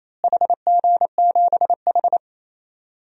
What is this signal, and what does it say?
Morse code, heard as a keyed tone, about 28 wpm, 700 Hz. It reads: HG7H
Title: DXCC-100 Call Signs - 4 Characters - Repeated 3x - 28wpm